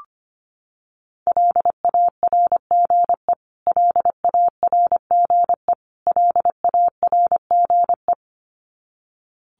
Morse code, heard as a keyed tone, about 25 wpm, 700 Hz. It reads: LARGE LARGE LARGE
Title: Top 200 Words Repeated 25wpm